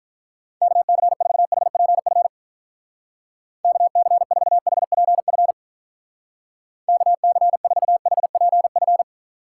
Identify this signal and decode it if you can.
Morse code, heard as a keyed tone, about 45 wpm, 700 Hz. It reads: KC4HPF KC4HPF KC4HPF